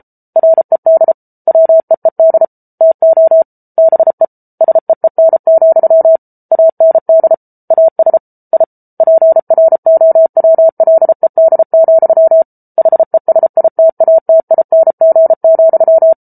Morse code, heard as a keyed tone, about 34 wpm, 650 Hz. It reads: RED WEED TO BE SEEN, AND AS I PROWLED, HESITATING,